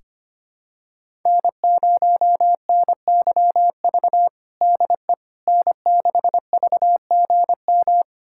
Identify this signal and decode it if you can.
Morse code, heard as a keyed tone, about 25 wpm, 700 Hz. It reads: N0NYV DE N6VGM
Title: USA Callsign DE Callsign - 5 Characters - 25wpm